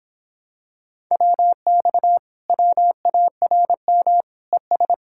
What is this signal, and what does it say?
Morse code, heard as a keyed tone, about 26 wpm, 700 Hz. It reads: WX WARM ES